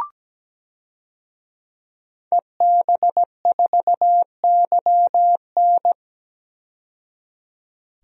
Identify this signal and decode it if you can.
Morse code, heard as a keyed tone, about 17 wpm, 700 Hz. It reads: EB4YN
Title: DXCC-100 Call Signs - 5 Characters - 17wpm